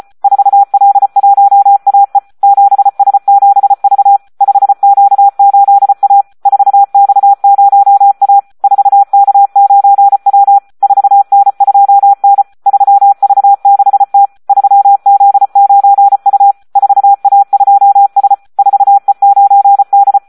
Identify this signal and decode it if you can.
Morse code, heard as a keyed tone, about 34 wpm, 800 Hz. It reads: 4L1AE 7S7V 5Q8A 4X0A 4K9W 4N2N 3V6T 3Z9U 4A2S 4E9D
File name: Audio practica CW / SD 1 34